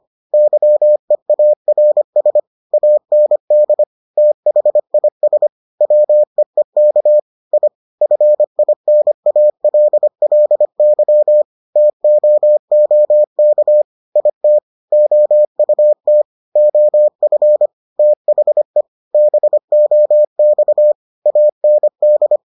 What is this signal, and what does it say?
Morse code, heard as a keyed tone, about 25 wpm, 600 Hz. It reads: YEARS AND THIS WEEK I FINALLY TOOK IT OUT OF THE BOX AND